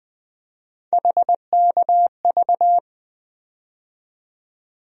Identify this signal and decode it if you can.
Morse code, heard as a keyed tone, about 21 wpm, 700 Hz. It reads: HKV
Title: USA Call Sign Postfixes - 3 Letters 20wpm